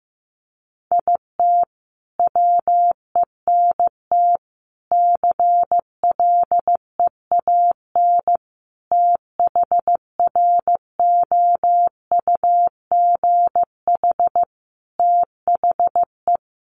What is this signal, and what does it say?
Morse code, heard as a keyed tone, about 15 wpm, 700 Hz. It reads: IT WENT CLEAN THROUGH THE
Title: Sentences from Top 1200 Words 15wpm